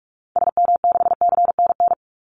Morse code, heard as a keyed tone, about 44 wpm, 700 Hz. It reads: SM6CNN
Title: Call Signs Repeated 3x - 45wpm